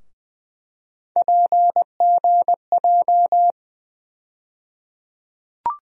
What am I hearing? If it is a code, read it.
Morse code, heard as a keyed tone, about 20 wpm, 700 Hz. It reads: PGJ
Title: USA Call Sign Postfixes - 3 Letters 20wpm